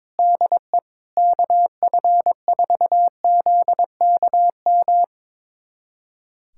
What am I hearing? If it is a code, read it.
Morse code, heard as a keyed tone, about 22 wpm, 700 Hz. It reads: DE KF4ZKM
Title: USA Callsign DE Callsign 22wpm